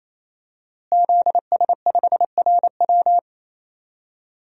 Morse code, heard as a keyed tone, about 27 wpm, 700 Hz. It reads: ZS5RW
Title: DXCC-100 Call Signs - 5 Characters - Repeated 3x - 28wpm